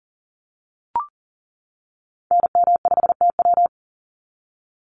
Morse code, heard as a keyed tone, about 40 wpm, 700 Hz. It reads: NM5TW
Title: NA SOTA Activator Callsigns 40wpm